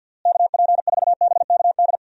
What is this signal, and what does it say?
Morse code, heard as a keyed tone, about 52 wpm, 700 Hz. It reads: KP4BKD
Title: DXCC-100 Call Signs - 6 Characters - Repeated 3x - 50wpm